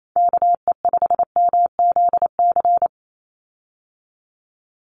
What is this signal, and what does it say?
Morse code, heard as a keyed tone, about 28 wpm, 700 Hz. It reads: KE5MZC
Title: USA Call Signs 28wpm